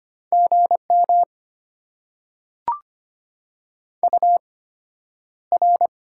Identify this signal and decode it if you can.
Morse code, heard as a keyed tone, about 25 wpm, spaced as for 11 wpm, 700 Hz. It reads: GM UR
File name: Top 90 QSO Elements - 8x Character Spacing - 25wpm